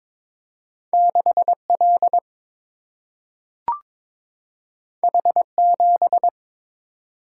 Morse code, heard as a keyed tone, about 22 wpm, 700 Hz. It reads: 6 L H 7
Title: Single-Letter Single-Number 22wpm